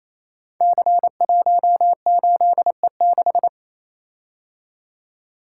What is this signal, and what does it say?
Morse code, heard as a keyed tone, about 28 wpm, 700 Hz. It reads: C18E6
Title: Code Groups - 5 Letter:Numbers - 28wpm